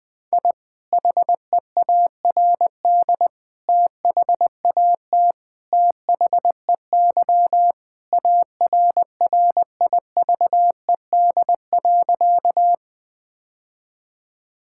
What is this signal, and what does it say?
Morse code, heard as a keyed tone, about 20 wpm, 700 Hz. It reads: I HEARD THAT THEY ARRIVED.
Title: Sentences with 2-Word Phrases - Spoken Beginning - 20wpm